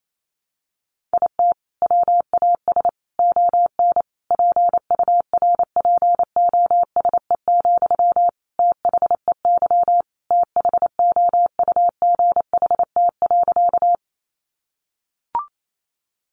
Morse code, heard as a keyed tone, about 28 wpm, 700 Hz. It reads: IT WAS ON PURPOSE, THEY THOUGHT.
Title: Sentences from Top 800 Words 28wpm